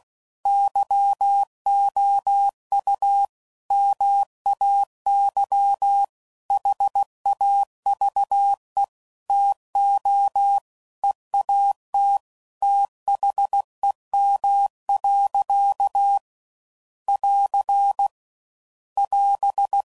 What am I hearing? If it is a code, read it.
Morse code, heard as a keyed tone, about 16 wpm, 800 Hz. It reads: YOU MAY HAVE TO EAT THEM. <AR> <AS>